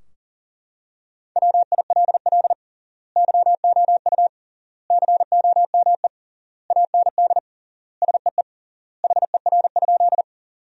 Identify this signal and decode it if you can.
Morse code, heard as a keyed tone, about 40 wpm, 700 Hz. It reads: WILL YOU COME AND SEE HER?
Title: 3x Word-Spacing - Sentences from Top 100 Words - 40wpm